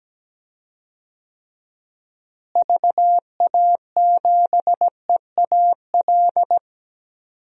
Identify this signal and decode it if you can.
Morse code, heard as a keyed tone, about 17 wpm, 700 Hz. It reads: VA7EAL
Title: DXCC-100 Call Signs - 6 Characters - Repeated 3x - 17wpm